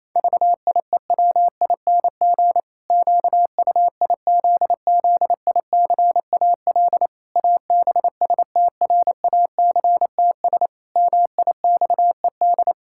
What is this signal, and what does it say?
Morse code, heard as a keyed tone, about 28 wpm, 700 Hz. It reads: VIEWING QUIZZICAL ABSTRACTS MIXED
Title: Pangram Challenge v2 - 28wpm